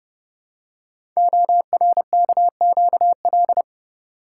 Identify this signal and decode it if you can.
Morse code, heard as a keyed tone, about 30 wpm, 700 Hz. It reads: ORKQL
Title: Code Groups - 5 Letters - 30wpm